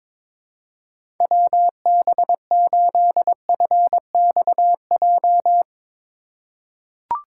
Morse code, heard as a keyed tone, about 21 wpm, 700 Hz. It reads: WB8FXJ
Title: K1USN SST QSO Practice - 22wpm